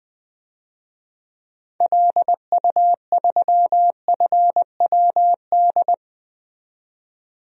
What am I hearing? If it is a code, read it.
Morse code, heard as a keyed tone, about 20 wpm, 700 Hz. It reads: LU3FWD
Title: DXCC-100 Call Signs - 6 Characters - 20wpm